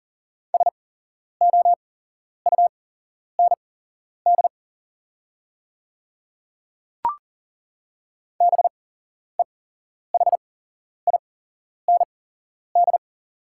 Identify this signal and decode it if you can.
Morse code, heard as a keyed tone, about 38 wpm, spaced as for 11 wpm, 690 Hz. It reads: SOUND BEHIND
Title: 8x Character Spacing - Top 500 Words - 40wpm